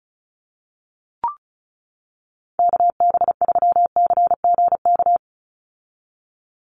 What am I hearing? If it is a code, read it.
Morse code, heard as a keyed tone, about 34 wpm, 700 Hz. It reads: KB3CGK
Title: USA Call Signs 35wpm